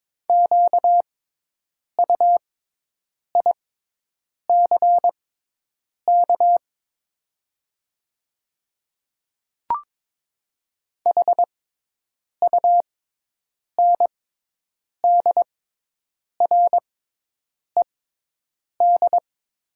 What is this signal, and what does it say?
Morse code, heard as a keyed tone, about 22 wpm, spaced as for 8 wpm, 700 Hz. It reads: QUICK HUNDRED
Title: 6x Character Spacing - Top 500 Words - 22wpm